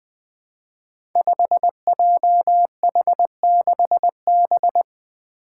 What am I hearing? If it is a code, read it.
Morse code, heard as a keyed tone, about 20 wpm, 700 Hz. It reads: 5JH6B
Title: Five Challenging Letter-Number Combinations 20wpm